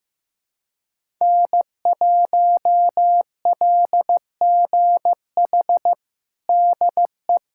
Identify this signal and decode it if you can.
Morse code, heard as a keyed tone, about 15 wpm, 700 Hz. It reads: N1LGH DE
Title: USA Callsign DE Callsign - 5 Characters - 15wpm